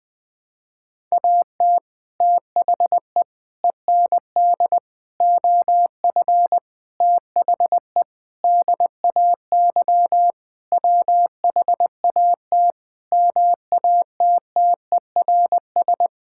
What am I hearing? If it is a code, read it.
Morse code, heard as a keyed tone, about 20 wpm, 700 Hz. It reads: AT THE END OF THE DAY WHAT MATTERS
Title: Sentences with 6-Word Phrases - Spoken Beginning - 20wpm